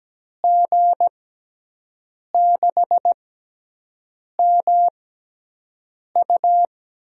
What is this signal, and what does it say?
Morse code, heard as a keyed tone, about 17 wpm, spaced as for 6 wpm, 700 Hz. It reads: G6MU
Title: USA Call Signs - 5 Characters - 6x Character Spacing - 17wpm